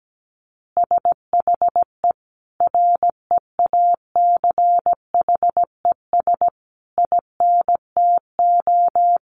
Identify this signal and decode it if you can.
Morse code, heard as a keyed tone, about 17 wpm, 700 Hz. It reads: SHE REACHES INTO
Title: Additional Easy Phrases 17wpm